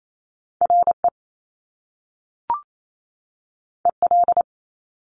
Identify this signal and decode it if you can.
Morse code, heard as a keyed tone, about 28 wpm, 700 Hz. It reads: RE EL
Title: Top Two Letter Combinations in Words 28wpm